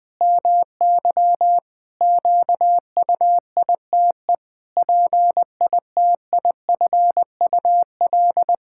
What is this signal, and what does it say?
Morse code, heard as a keyed tone, about 20 wpm, 700 Hz. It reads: MY QUITE PITIFUL
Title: Pangram Challenge v2 - 20wpm